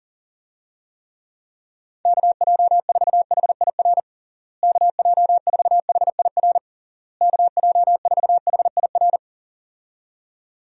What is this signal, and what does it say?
Morse code, heard as a keyed tone, about 40 wpm, 700 Hz. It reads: KJ4HIR KJ4HIR KJ4HIR
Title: USA Call Signs - 6 Characters - Repeated 3x - 40wpm